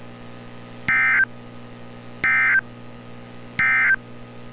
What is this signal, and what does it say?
Emergency Alert System (SAME) tones, heard as harsh buzzing bursts. It is heard from 0.8 to 4.0 s.
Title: NOAA Weather Radio - Altoona, PA